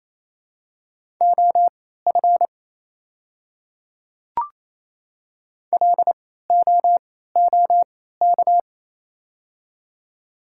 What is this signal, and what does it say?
Morse code, heard as a keyed tone, about 28 wpm, spaced as for 16 wpm, 700 Hz. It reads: OF LOOK
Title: 3x Character Spacing - Top 200 Words - 28wpm